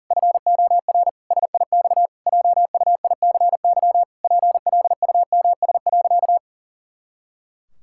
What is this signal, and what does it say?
Morse code, heard as a keyed tone, about 37 wpm, 700 Hz. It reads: FOR SIX JUICY PLUMS.
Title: Pangram Challenge v2 - 40wpm